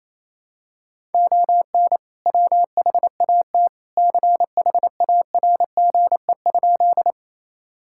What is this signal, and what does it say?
Morse code, heard as a keyed tone, about 28 wpm, 700 Hz. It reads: ON WHAT CHARGE?
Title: Sentences from Top 1000 Words - Review Part 6 - 28wpm